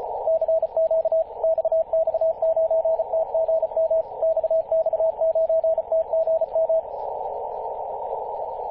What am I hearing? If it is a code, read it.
Morse code, heard as a keyed tone, about 34 wpm, 660 Hz, with radio noise. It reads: CQ XX9TGM XX9TGM
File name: posts / 2016-june-21-0802z-xx9tgm-14-028mhz-cw